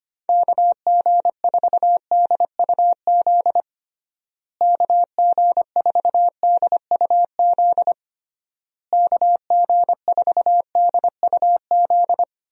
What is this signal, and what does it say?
Morse code, heard as a keyed tone, about 25 wpm, 700 Hz. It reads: KG4DUZ KG4DUZ KG4DUZ
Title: USA Call Signs - 6 Characters - Repeated 3x - 25wpm